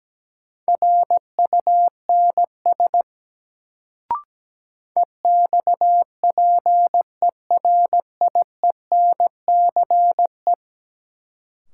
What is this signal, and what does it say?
Morse code, heard as a keyed tone, about 17 wpm, 700 Hz. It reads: RUNS EXPERIENCE
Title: Sentences from Top 600 Words - New Word Drill - 17wpm